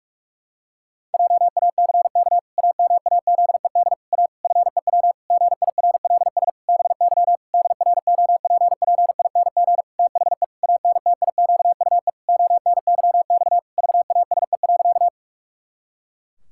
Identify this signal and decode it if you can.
Morse code, heard as a keyed tone, about 46 wpm, 700 Hz. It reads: JACK AMAZED A FEW GIRLS BY DROPPING THE ANTIQUE ONYX VASE.
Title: Pangram Challenge v2 - 45wpm